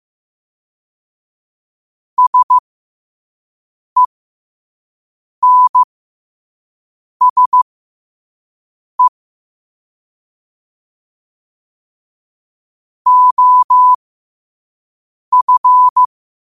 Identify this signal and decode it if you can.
Morse code, heard as a keyed tone, about 15 wpm, spaced as for 5 wpm, 1000 Hz. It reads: SENSE OF
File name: qotdfiles / 5wpm